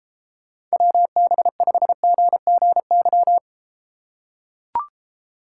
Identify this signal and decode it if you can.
Morse code, heard as a keyed tone, about 34 wpm, 700 Hz. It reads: WB5GGY